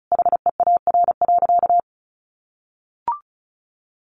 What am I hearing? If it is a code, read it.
Morse code, heard as a keyed tone, about 34 wpm, 700 Hz. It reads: HEAR.
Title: A Sound Of Thunder 35wpm